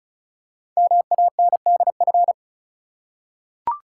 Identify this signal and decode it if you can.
Morse code, heard as a keyed tone, about 36 wpm, 690 Hz. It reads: MANDF
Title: Code Groups - 5 Letters - 35wpm